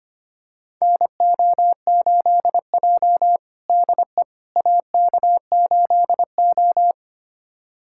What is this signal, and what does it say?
Morse code, heard as a keyed tone, about 25 wpm, 700 Hz. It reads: NO8J DE AK8O
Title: USA Callsign DE Callsign - 4 Characters - 25wpm